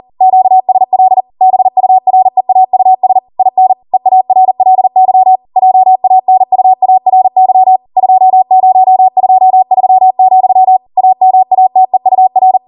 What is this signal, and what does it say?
Morse code, heard as a keyed tone, about 40 wpm, 750 Hz. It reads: QSL BUREAUS IN EARLY JANUARY 2023, AMATEUR